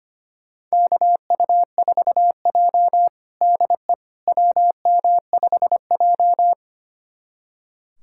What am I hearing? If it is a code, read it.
Morse code, heard as a keyed tone, about 25 wpm, 700 Hz. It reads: KU4J DE WM5J
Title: USA Callsign DE Callsign - 4 Characters - 25wpm